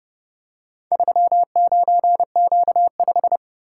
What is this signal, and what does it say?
Morse code, heard as a keyed tone, about 30 wpm, 700 Hz. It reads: 39Q5